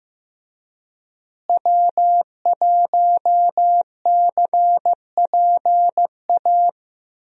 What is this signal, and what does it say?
Morse code, heard as a keyed tone, about 15 wpm, 700 Hz. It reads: W1CPA
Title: USA Call Signs - 5 Characters - Repeated 2x - 15wpm